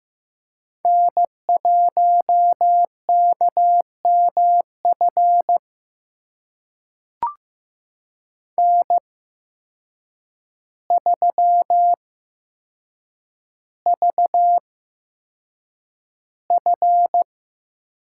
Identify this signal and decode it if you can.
Morse code, heard as a keyed tone, about 15 wpm, spaced as for 7 wpm, 700 Hz. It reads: N1KMF N3VF
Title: USA Call Signs - 5 Characters - 8x Character Spacing - 15wpm